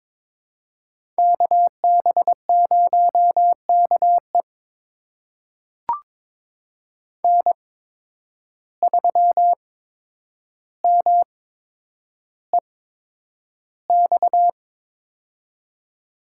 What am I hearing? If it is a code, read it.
Morse code, heard as a keyed tone, about 22 wpm, spaced as for 9 wpm, 700 Hz. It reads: KB0KE N3MEX
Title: USA Call Signs - 5 Characters - 8x Character Spacing - 22wpm